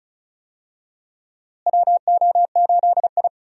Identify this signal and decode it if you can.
Morse code, heard as a keyed tone, about 34 wpm, 700 Hz. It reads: WO8I
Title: USA Call Signs - 4 Characters - Repeated 3x - 35wpm